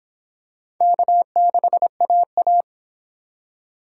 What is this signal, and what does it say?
Morse code, heard as a keyed tone, about 26 wpm, 700 Hz. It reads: K6AA